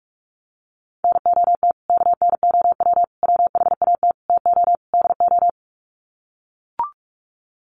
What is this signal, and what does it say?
Morse code, heard as a keyed tone, about 44 wpm, 700 Hz. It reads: NOT KNOW WHAT TO DO
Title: Sets of 5 Words 45wpm